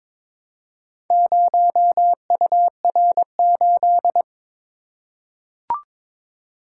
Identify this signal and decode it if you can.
Morse code, heard as a keyed tone, about 22 wpm, 700 Hz. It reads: 0UR8
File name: practice / 22wpm / Four Letter-Number Combinations - 22wpm